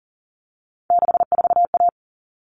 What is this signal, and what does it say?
Morse code, heard as a keyed tone, about 40 wpm, 700 Hz. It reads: 64A
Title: All Cut Numbers - 3-Digit 40wpm